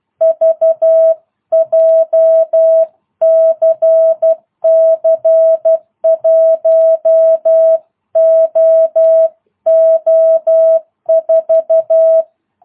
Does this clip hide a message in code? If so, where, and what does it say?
Yes: Morse code from 0.2 to 12.2 s, heard as a keyed tone, about 12 wpm, 650 Hz. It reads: V JCC 1OO4